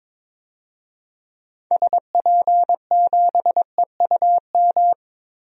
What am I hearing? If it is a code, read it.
Morse code, heard as a keyed tone, about 22 wpm, 700 Hz. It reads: SP7EUM
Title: DXCC-100 Call Signs - 6 Characters - Repeated 2x - 22wpm